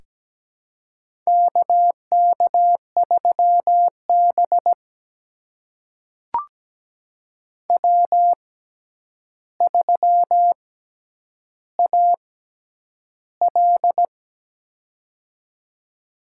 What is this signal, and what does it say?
Morse code, heard as a keyed tone, about 17 wpm, spaced as for 9 wpm, 700 Hz. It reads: KK3B W3AL
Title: USA Call Signs - 4 Characters - 6x Character Spacing - 17wpm